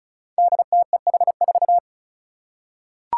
Morse code, heard as a keyed tone, about 36 wpm, 700 Hz. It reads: DTEH4
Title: Five Challenging Letter-Number Combinations 35wpm